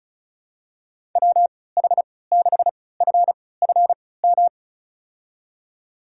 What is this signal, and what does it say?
Morse code, heard as a keyed tone, about 36 wpm, spaced as for 20 wpm, 700 Hz. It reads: WH6FFM
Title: USA Call Signs - 6 Characters - 3x Character Spacing - 35wpm